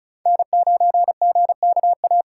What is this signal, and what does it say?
Morse code, heard as a keyed tone, about 36 wpm, 700 Hz. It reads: N9GKA
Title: USA Call Signs - 4x Character Spacing - 35wpm